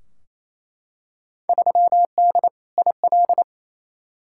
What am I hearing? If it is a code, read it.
Morse code, heard as a keyed tone, about 28 wpm, 700 Hz. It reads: 3D IL
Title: Field Day Exchanges Encore - 28wpm